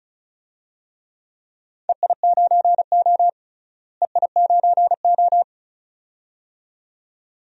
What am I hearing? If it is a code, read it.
Morse code, heard as a keyed tone, about 36 wpm, 700 Hz. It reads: EI9O EI9O
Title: DXCC-100 Call Signs - 4 Characters - Repeated 2x - 35wpm